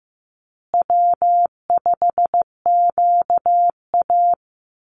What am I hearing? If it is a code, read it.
Morse code, heard as a keyed tone, about 15 wpm, 700 Hz. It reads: W5QA
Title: USA Call Signs Repeated 2x 15wpm